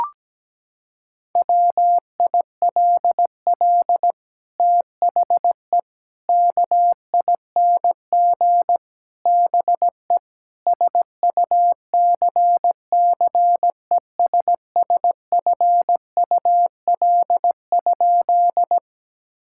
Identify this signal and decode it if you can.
Morse code, heard as a keyed tone, about 17 wpm, 700 Hz. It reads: WILL THE KING BE SUCCESSFUL?
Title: Sentences from Top 1000 Words - Review Part 6 - 17wpm